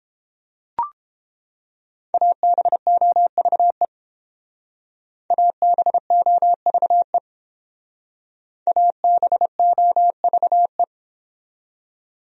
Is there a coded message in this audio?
Yes: Morse code from 2.1 to 10.8 s, heard as a keyed tone, about 28 wpm, 700 Hz. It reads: ABOVE ABOVE ABOVE